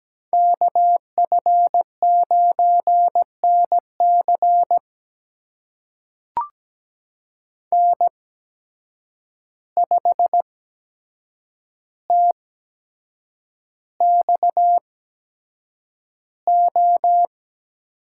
Morse code, heard as a keyed tone, about 17 wpm, spaced as for 7 wpm, 700 Hz. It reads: KF9NC N5TXO
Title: USA Call Signs - 5 Characters - 8x Character Spacing - 17wpm